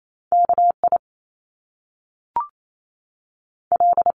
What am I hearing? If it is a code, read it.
Morse code, heard as a keyed tone, about 28 wpm, 700 Hz. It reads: KI L
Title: Top Two Letter Combinations in Words 28wpm